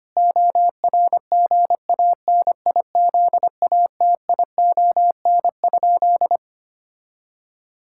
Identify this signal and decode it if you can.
Morse code, heard as a keyed tone, about 25 wpm, 700 Hz. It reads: ORGANIZATION?
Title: Sentences from Top 1000 Words - Review Part 5 - 25wpm